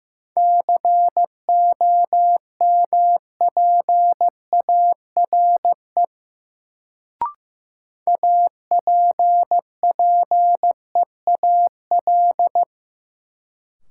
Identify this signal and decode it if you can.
Morse code, heard as a keyed tone, about 15 wpm, 700 Hz. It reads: COMPARE APPEAL
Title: Sentences from Top 1300 Words - New Word Drill Alt - 15wpm